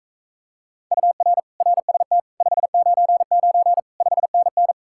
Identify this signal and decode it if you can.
Morse code, heard as a keyed tone, about 44 wpm, 700 Hz. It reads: UR RST 599 5NN